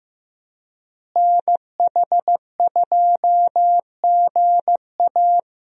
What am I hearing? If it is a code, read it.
Morse code, heard as a keyed tone, about 15 wpm, 700 Hz. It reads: NH2GA
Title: USA Call Signs - 5 Characters - Repeated 3x - 15wpm